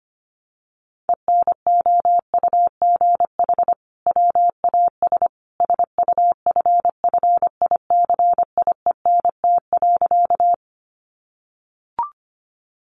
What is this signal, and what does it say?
Morse code, heard as a keyed tone, about 25 wpm, 700 Hz. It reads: ENOUGH WAS SUFFICIENT.
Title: Sentences from Top 1400 Words 25wpm